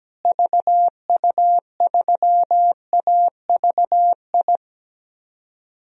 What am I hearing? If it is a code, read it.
Morse code, heard as a keyed tone, about 17 wpm, 700 Hz. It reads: VU3AVI
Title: DXCC-100 Call Signs - 6 Characters - Repeated 2x - 17wpm